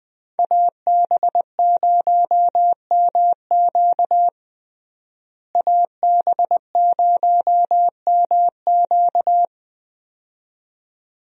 Morse code, heard as a keyed tone, about 20 wpm, 700 Hz. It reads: AB0MQ AB0MQ
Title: USA Call Signs - 5 Characters - Repeated 2x - 20wpm